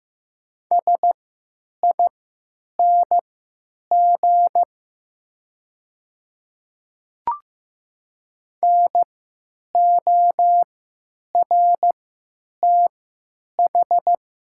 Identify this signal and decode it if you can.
Morse code, heard as a keyed tone, about 15 wpm, spaced as for 9 wpm, 700 Hz. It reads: SING NORTH
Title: 3x Character Spacing - Top 500 Words - 15wpm